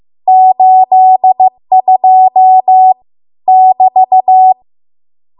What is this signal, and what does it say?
Morse code, heard as a keyed tone, about 15 wpm, 750 Hz. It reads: 82 <BT>